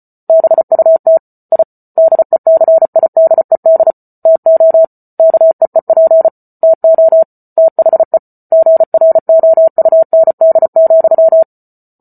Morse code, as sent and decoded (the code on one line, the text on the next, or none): -... ..- - / .. / -.. -.-. .. -.. -.. / - --- / -.- . .--. / - --- / - .... / --. .-. --- ..- -. -.. --..--
BUT I DECIDED TO KEEP TO THE GROUND,